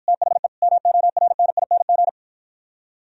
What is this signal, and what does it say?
Morse code, heard as a keyed tone, about 51 wpm, 700 Hz. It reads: THE MORNING